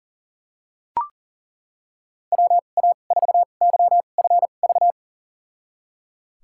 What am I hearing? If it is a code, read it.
Morse code, heard as a keyed tone, about 39 wpm, spaced as for 29 wpm, 690 Hz. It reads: WA4YFV
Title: USA Call Signs - 2x Character Spacing - 40wpm